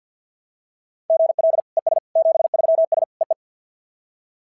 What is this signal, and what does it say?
Morse code, heard as a keyed tone, about 51 wpm, 690 Hz. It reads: GL ES 73S EE
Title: SOTA QSO Practice 50wpm